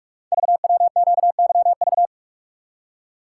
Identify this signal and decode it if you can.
Morse code, heard as a keyed tone, about 44 wpm, 700 Hz. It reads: VWQYV
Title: Code Groups - 5 Letters - 45wpm